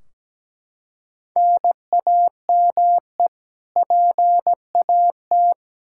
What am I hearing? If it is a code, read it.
Morse code, heard as a keyed tone, about 17 wpm, 700 Hz. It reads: NAME PAT
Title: Practice QSO - Exchange 1 - Beginning - 17wpm